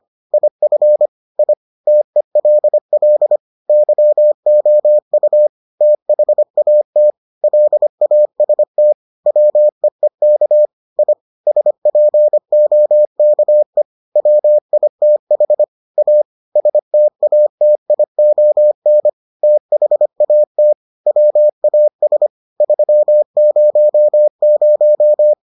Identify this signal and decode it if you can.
Morse code, heard as a keyed tone, about 25 wpm, 600 Hz. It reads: IF I TELL YOU THAT LAST WEEK I SPOKE WITH A STATION THAT WAS 300